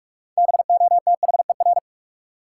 Morse code, heard as a keyed tone, about 44 wpm, 700 Hz. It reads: BOTHER
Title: Six Letter Words - 45wpm